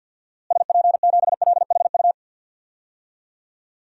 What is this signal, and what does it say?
Morse code, heard as a keyed tone, about 52 wpm, 700 Hz. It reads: SP7LHU
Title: DXCC-100 Call Signs - 6 Characters - Repeated 3x - 50wpm